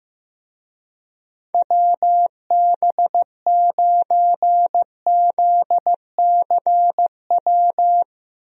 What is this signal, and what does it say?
Morse code, heard as a keyed tone, about 15 wpm, 700 Hz. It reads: WB9ZCW
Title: USA Call Signs - 6 Characters - Repeated 2x - 15wpm